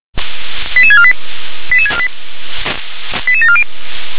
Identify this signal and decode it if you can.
Five-tone selective call (ZVEI), heard as quick runs of five tones, tones of about 70 ms. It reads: 88648 88648 88740